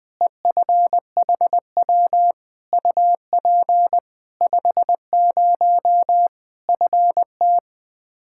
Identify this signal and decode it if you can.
Morse code, heard as a keyed tone, about 20 wpm, 700 Hz. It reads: EFHW UP 50 FT
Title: Practice QSO - Exchange 2 - Info - 20wpm